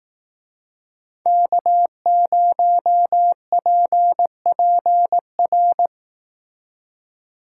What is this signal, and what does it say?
Morse code, heard as a keyed tone, about 18 wpm, 700 Hz. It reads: K0PPR
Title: Speed Racing - USA 1x3 Call Signs 15wpm